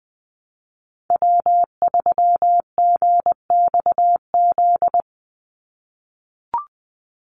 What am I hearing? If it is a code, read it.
Morse code, heard as a keyed tone, about 20 wpm, 700 Hz. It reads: W3GXZ
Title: USA Call Signs 20wpm